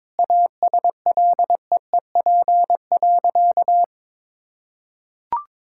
Morse code, heard as a keyed tone, about 22 wpm, 700 Hz. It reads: ASLEEP.
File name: practice / 22wpm / Sentences with 3-Word Phrases - 22wpm